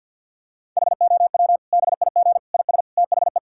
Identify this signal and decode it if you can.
Morse code, heard as a keyed tone, about 50 wpm, 700 Hz. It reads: HOW BIG IS THE
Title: Sentences from Top 1000 Words - Review Part 5 - 50wpm